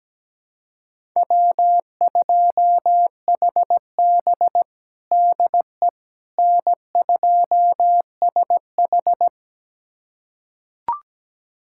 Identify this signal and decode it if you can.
Morse code, heard as a keyed tone, about 17 wpm, 700 Hz. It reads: W2HB DE N2SH
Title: USA Callsign DE Callsign - 4 Characters - 17wpm